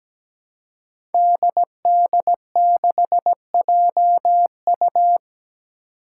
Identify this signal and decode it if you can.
Morse code, heard as a keyed tone, about 17 wpm, 700 Hz. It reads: DD6JU
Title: Five Challenging Letter-Number Combinations 17wpm